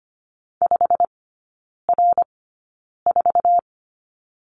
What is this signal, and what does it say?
Morse code, heard as a keyed tone, about 25 wpm, spaced as for 9 wpm, 690 Hz. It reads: 5R4
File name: Lesson 15e - Number 5 - Three Characters 25wpm